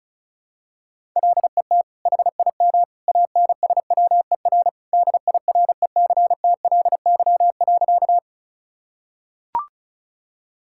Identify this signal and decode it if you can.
Morse code, heard as a keyed tone, about 36 wpm, 700 Hz. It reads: LET HIM ANSWER DIRECTLY.
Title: Sentences from Top 1000 Words - Review Part 5 - 35wpm